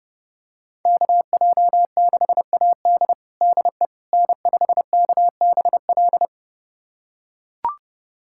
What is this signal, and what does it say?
Morse code, heard as a keyed tone, about 30 wpm, 700 Hz. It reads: KJ6AD DE N5KBL
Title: USA Callsign DE Callsign - 5 Characters - 30wpm